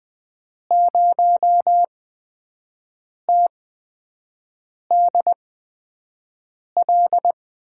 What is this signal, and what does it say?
Morse code, heard as a keyed tone, about 20 wpm, spaced as for 5 wpm, 700 Hz. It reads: 0TDL
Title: USA Call Signs - 8x Character Spacing - 20wpm